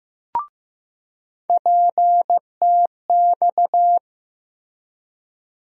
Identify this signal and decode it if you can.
Morse code, heard as a keyed tone, about 15 wpm, 700 Hz. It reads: PTX
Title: USA Call Sign Postfixes - 3 Letters 15wpm